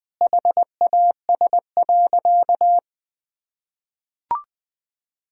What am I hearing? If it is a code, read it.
Morse code, heard as a keyed tone, about 20 wpm, 700 Hz. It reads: HAS.
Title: Sentences from Top 1000 Words - Review Part 2 - 20wpm